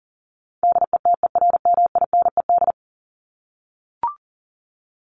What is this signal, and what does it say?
Morse code, heard as a keyed tone, about 40 wpm, 700 Hz. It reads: DETERMINED
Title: ED Suffix - Top 100 Words with Base Words 40wpm